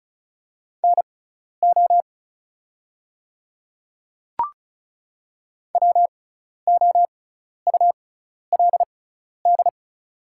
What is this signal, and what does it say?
Morse code, heard as a keyed tone, about 36 wpm, spaced as for 12 wpm, 690 Hz. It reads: NO WOULD
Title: 6x Character Spacing - Top 300 Words - 35wpm